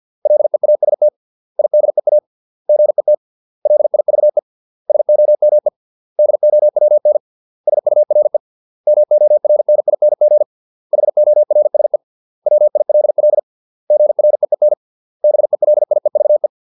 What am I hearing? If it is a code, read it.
Morse code, heard as a keyed tone, about 50 wpm, 600 Hz. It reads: LEAST IDEA GET LIVE SOME DOWN SURE MORNING HORSE WILL GREEN BELIEVE